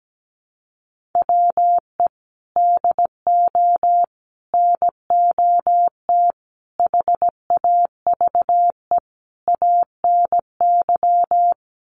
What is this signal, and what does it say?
Morse code, heard as a keyed tone, about 17 wpm, 700 Hz. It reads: WE DO NOT HAVE ANY
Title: Sets of 5 Words 17wpm